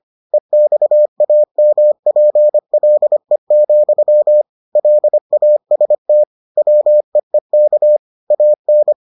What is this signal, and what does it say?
Morse code, heard as a keyed tone, about 25 wpm, 600 Hz. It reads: EXAMPLE, LAST WEEK AN